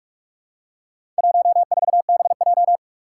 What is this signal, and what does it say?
Morse code, heard as a keyed tone, about 44 wpm, 700 Hz. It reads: 14BJ
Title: Four Challenging Letter-Number Combinations 45wpm